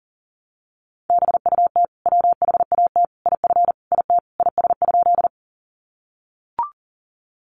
Morse code, heard as a keyed tone, about 40 wpm, 700 Hz. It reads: BUT WHAT IF IT IS?
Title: Sentences from Top 100 Words 40wpm